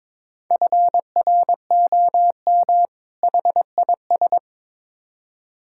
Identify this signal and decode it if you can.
Morse code, heard as a keyed tone, about 22 wpm, 700 Hz. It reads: FROM HIS
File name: Build A Phrase - 2-Word Phrases - 22wpm